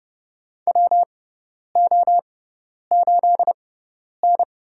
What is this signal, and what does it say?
Morse code, heard as a keyed tone, about 30 wpm, spaced as for 10 wpm, 700 Hz. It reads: WO8N
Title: USA Call Signs - 6x Character Spacing - 30wpm